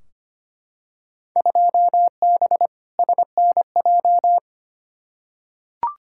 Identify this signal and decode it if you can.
Morse code, heard as a keyed tone, about 25 wpm, 700 Hz. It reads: 2B SNJ
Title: Field Day Exchanges Encore - 25wpm